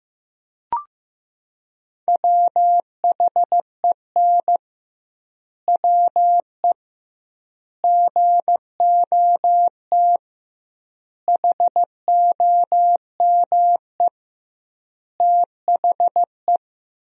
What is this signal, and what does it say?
Morse code, heard as a keyed tone, about 15 wpm, 700 Hz. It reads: WHEN WE GOT HOME THE
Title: 2x Word-Spacing - Sentences from Top 500 Words - 15wpm